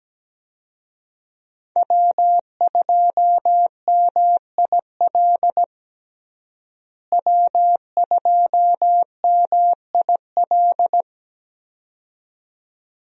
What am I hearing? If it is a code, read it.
Morse code, heard as a keyed tone, about 17 wpm, 700 Hz. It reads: W2MIL W2MIL
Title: USA Call Signs - 5 Characters - Repeated 2x - 17wpm